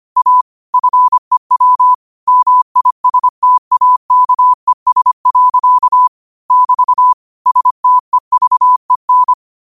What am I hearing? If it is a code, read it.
Morse code, heard as a keyed tone, about 25 wpm, 1000 Hz. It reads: A FEW MISTAKES. <BT> STEVEN
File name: 25wpm